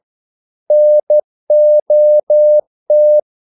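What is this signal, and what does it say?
Morse code, heard as a keyed tone, about 12 wpm, 600 Hz. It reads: NOT
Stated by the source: Morse practice sending English quotations.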